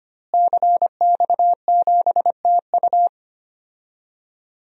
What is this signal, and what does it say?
Morse code, heard as a keyed tone, about 25 wpm, 700 Hz. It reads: CX7TU
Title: DXCC-100 Call Signs - 5 Characters - Repeated 3x - 25wpm